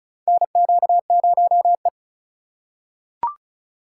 Morse code, heard as a keyed tone, about 36 wpm, 700 Hz. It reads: NQ0E
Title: USA Call Signs - 4 Characters - 3x Character Spacing - 35wpm